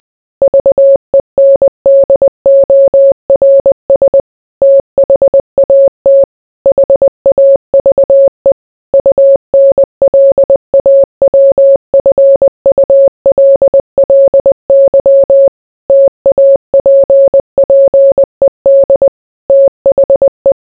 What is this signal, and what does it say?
Morse code, heard as a keyed tone, about 20 wpm, 550 Hz. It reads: VENDORS THAT HAVE UNLAWFULLY TAPPED THE